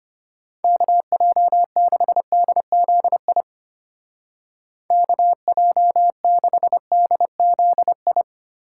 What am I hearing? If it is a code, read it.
Morse code, heard as a keyed tone, about 31 wpm, 700 Hz. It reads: KJ6DZI KJ6DZI
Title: Speed Racing - USA Call Signs 25wpm